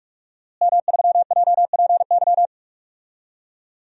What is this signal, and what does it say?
Morse code, heard as a keyed tone, about 42 wpm, 700 Hz. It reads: M3JPY
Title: DXCC-100 Call Signs - 5 Characters - Repeated 3x - 45wpm